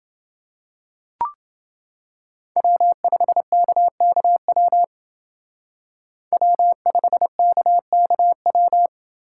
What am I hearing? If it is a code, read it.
Morse code, heard as a keyed tone, about 28 wpm, 700 Hz. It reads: W5KKW W5KKW